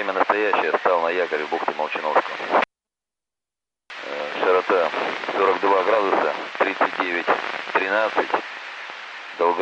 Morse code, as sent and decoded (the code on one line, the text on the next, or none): none